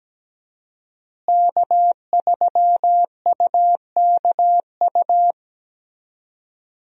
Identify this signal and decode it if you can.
Morse code, heard as a keyed tone, about 17 wpm, 700 Hz. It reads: K3UKU
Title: USA Call Signs - 5 Characters - Repeated 3x - 17wpm